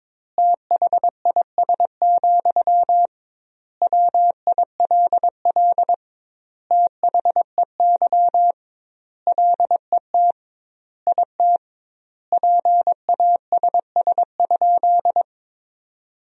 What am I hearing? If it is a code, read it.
Morse code, heard as a keyed tone, about 22 wpm, 700 Hz. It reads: THIS, WILL THEY LET IT PASS?
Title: 2x Word-Spacing - Sentences from Top 400 Words - 22wpm